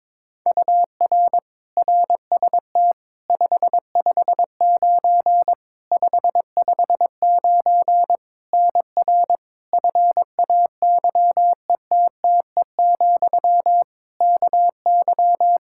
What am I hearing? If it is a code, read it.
Morse code, heard as a keyed tone, about 22 wpm, 700 Hz. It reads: UR RST 559 559 NR FAYETTE, KY
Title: Practice QSO - Exchange 1 - Beginning - 22wpm